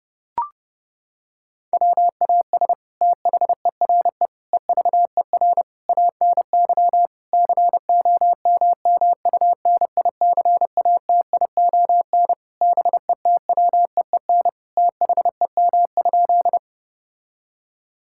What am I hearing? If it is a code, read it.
Morse code, heard as a keyed tone, about 30 wpm, 700 Hz. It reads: WAS THERE EVER ANY COMMUNICATION BETWEEN THEM?
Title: Sentences from Top 1000 Words - Review Part 6 - 30wpm